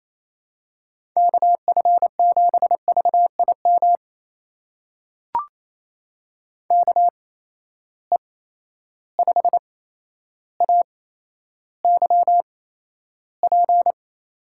Morse code, heard as a keyed tone, about 28 wpm, spaced as for 12 wpm, 700 Hz. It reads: KF7VIM KE5AYP
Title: USA Call Signs - 6 Characters - 8x Character Spacing - 28wpm